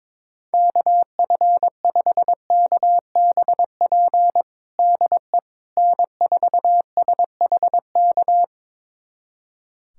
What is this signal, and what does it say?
Morse code, heard as a keyed tone, about 21 wpm, 700 Hz. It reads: KF5KBP DE N4SHK
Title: USA Callsign DE Callsign 22wpm